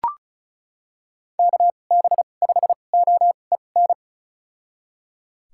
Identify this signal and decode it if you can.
Morse code, heard as a keyed tone, about 34 wpm, spaced as for 25 wpm, 700 Hz. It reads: KB5OEN
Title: USA Call Signs - 2x Character Spacing - 35wpm